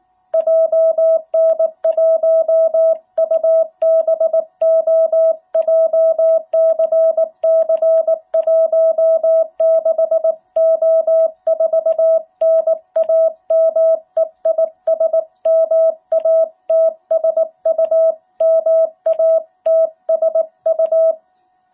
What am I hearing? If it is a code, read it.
Morse code, heard as a keyed tone, about 19 wpm, 660 Hz. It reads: JN1UBO JCC16O4 NAME IS MATSU MATSU
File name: jh8cbh / 2021cw